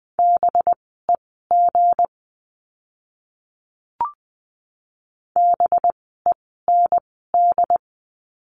Morse code, heard as a keyed tone, about 20 wpm, spaced as for 15 wpm, 700 Hz. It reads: BEG BEND
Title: 185 Lesson 30f - Letter b - Words 20wpm